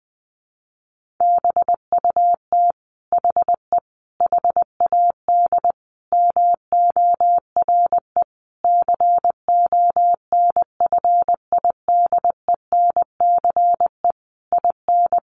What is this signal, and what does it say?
Morse code, heard as a keyed tone, about 20 wpm, 700 Hz. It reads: BUT HE HAD MORE CONFIDENCE IN
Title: Sentences from Top 1500 Words 20wpm